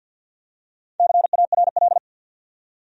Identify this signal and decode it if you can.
Morse code, heard as a keyed tone, about 53 wpm, 700 Hz. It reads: CARL
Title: USA Common Names - Max 5 Letters - 50wpm